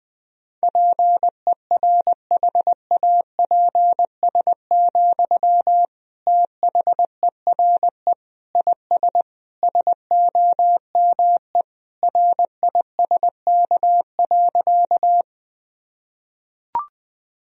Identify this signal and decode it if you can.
Morse code, heard as a keyed tone, about 21 wpm, 700 Hz. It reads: PERHAPS, THERE IS SOME RISK.
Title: Sentences from Top 1000 Words - Review Part 4 - 20wpm